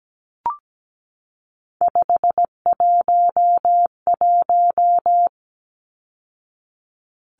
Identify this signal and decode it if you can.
Morse code, heard as a keyed tone, about 17 wpm, 700 Hz. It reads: 511
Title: Numbers - 3-Digit 17wpm